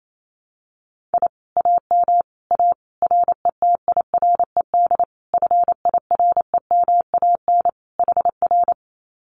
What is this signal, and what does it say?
Morse code, heard as a keyed tone, about 28 wpm, 700 Hz. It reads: I AM A RETIRED FIREMAN HR
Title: Common QSO Phrases Encore 28wpm